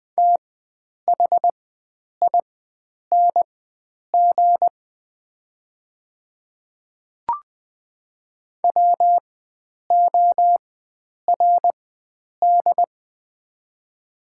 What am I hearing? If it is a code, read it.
Morse code, heard as a keyed tone, about 20 wpm, spaced as for 9 wpm, 700 Hz. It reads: THING WORD
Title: 4x Character Spacing - Top 100 Words - 20wpm